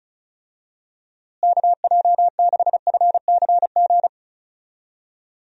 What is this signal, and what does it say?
Morse code, heard as a keyed tone, about 34 wpm, 700 Hz. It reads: KJ6FCG
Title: USA Call Signs - 6 Characters - Repeated 3x - 35wpm